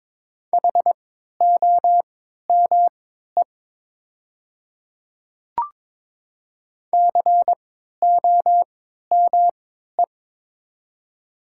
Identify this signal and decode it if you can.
Morse code, heard as a keyed tone, about 23 wpm, spaced as for 13 wpm, 700 Hz. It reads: HOME COME
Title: 3x Character Spacing - Top 200 Words - 22wpm